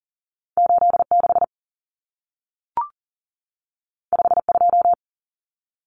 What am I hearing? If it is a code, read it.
Morse code, heard as a keyed tone, about 40 wpm, 700 Hz. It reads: 86 52
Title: Common Cut Numbers - 2-Digit 40wpm